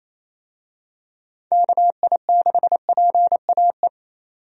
Morse code, heard as a keyed tone, about 28 wpm, 700 Hz. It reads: KI6PAE
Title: USA Call Signs - 6 Characters - 28wpm